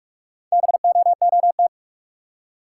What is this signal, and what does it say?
Morse code, heard as a keyed tone, about 45 wpm, 700 Hz. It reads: BOOT
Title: Four Letter Words - 45wpm